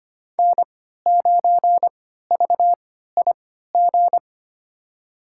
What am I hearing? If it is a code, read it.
Morse code, heard as a keyed tone, about 25 wpm, spaced as for 14 wpm, 700 Hz. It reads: N9VIG
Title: USA Call Signs - 3x Character Spacing - 25wpm